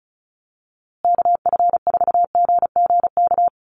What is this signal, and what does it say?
Morse code, heard as a keyed tone, about 34 wpm, 700 Hz. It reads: KF4GGK
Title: USA Call Signs Repeated 2x 35wpm